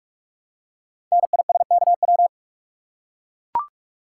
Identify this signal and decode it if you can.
Morse code, heard as a keyed tone, about 44 wpm, 690 Hz. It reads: NISKW
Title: Code Groups - 5 Letters - 45wpm